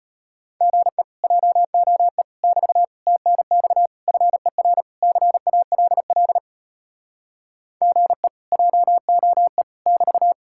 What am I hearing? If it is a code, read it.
Morse code, heard as a keyed tone, about 33 wpm, 700 Hz. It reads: GE JOE <BT> TNX FER CALL GE JOE <BT>